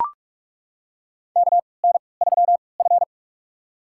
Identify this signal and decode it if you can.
Morse code, heard as a keyed tone, about 42 wpm, spaced as for 25 wpm, 700 Hz. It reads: KN3F
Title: USA Call Signs - 4 Characters - 3x Character Spacing - 45wpm